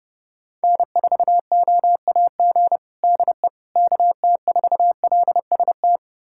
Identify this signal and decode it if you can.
Morse code, heard as a keyed tone, about 32 wpm, 700 Hz. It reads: N4OAG DE KT4LST
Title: USA Callsign DE Callsign 30wpm